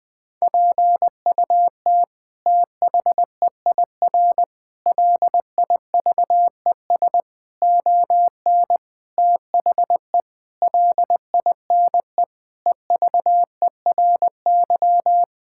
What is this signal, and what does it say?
Morse code, heard as a keyed tone, about 21 wpm, 700 Hz. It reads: PUT THEIR LIVES ON THE LINE EVERY
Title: Sets of 8 Words - 20wpm